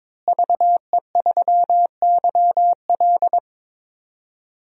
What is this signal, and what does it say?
Morse code, heard as a keyed tone, about 22 wpm, 700 Hz. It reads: VE3YL
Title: DXCC-100 Call Signs - 5 Characters - Repeated 3x - 22wpm